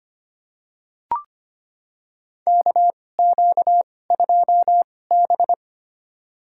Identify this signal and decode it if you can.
Morse code, heard as a keyed tone, about 25 wpm, spaced as for 18 wpm, 700 Hz. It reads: KQ2B
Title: USA Call Signs - 4 Characters - 2x Character Spacing - 25wpm